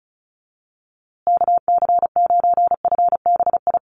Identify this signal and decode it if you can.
Morse code, heard as a keyed tone, about 34 wpm, 700 Hz. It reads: KC9FBI
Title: USA Call Signs Repeated 2x 35wpm